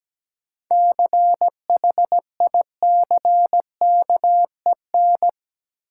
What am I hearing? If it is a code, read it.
Morse code, heard as a keyed tone, about 17 wpm, 700 Hz. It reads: CHICKEN
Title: Sentences from Top 1000 Words - Review Part 5 - New Word Drill - 17wpm